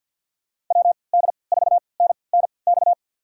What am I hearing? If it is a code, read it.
Morse code, heard as a keyed tone, about 51 wpm, spaced as for 28 wpm, 700 Hz. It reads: WD4NNX
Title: USA Call Signs - 3x Character Spacing - 50wpm